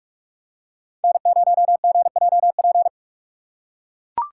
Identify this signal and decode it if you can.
Morse code, heard as a keyed tone, about 42 wpm, 700 Hz. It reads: N0GJP
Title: USA Call Signs - 5 Characters - 6x Character Spacing - 45wpm